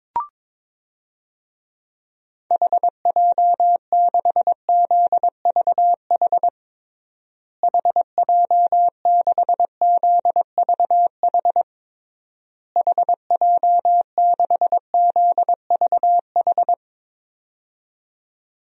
Morse code, heard as a keyed tone, about 22 wpm, 700 Hz. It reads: HJ6ZVH HJ6ZVH HJ6ZVH
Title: DXCC-100 Call Signs - 6 Characters - Repeated 3x - 22wpm